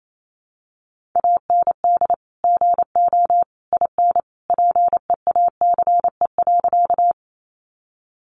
Morse code, heard as a keyed tone, about 28 wpm, 700 Hz. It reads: AND GO IN PEACE.
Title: Sentences from Top 1500 Words 28wpm